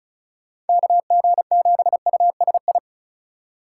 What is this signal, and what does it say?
Morse code, heard as a keyed tone, about 36 wpm, 700 Hz. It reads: KG7USI
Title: USA Call Signs - 6 Characters - 3x Character Spacing - 35wpm